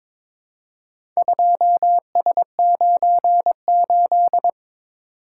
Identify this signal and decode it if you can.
Morse code, heard as a keyed tone, about 22 wpm, 700 Hz. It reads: 2S98
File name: practice / 22wpm / Four Letter-Number Combinations - 22wpm